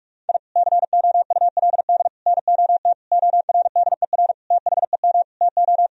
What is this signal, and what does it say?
Morse code, heard as a keyed tone, about 46 wpm, 700 Hz. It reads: I COULD NOT ORDER THEM TO
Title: Sentences from Top 1000 Words - Review Part 2 - 45wpm